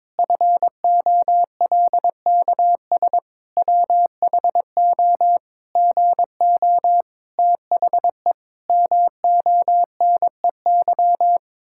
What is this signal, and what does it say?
Morse code, heard as a keyed tone, about 22 wpm, 700 Hz. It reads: FOLKS WHO GO THE MONEY
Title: Pangram Challenge v2 - 22wpm